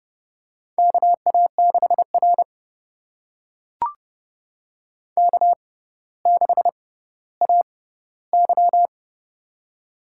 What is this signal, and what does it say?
Morse code, heard as a keyed tone, about 30 wpm, spaced as for 15 wpm, 700 Hz. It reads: KA6R K6AY
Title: USA Call Signs - 4 Characters - 6x Character Spacing - 30wpm